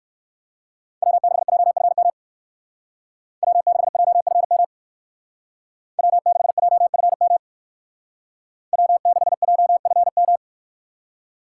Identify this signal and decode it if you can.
Morse code, heard as a keyed tone, about 63 wpm, 700 Hz. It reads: W6JFM W6JFM W6JFM W6JFM